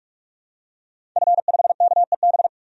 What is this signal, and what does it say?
Morse code, heard as a keyed tone, about 46 wpm, 700 Hz. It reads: F5KEB
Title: DXCC-100 Call Signs - 5 Characters - 45wpm